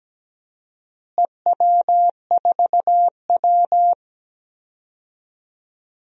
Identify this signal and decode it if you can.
Morse code, heard as a keyed tone, about 17 wpm, 700 Hz. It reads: EW4W
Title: DXCC-100 Call Signs - 4 Characters - Repeated 3x - 17wpm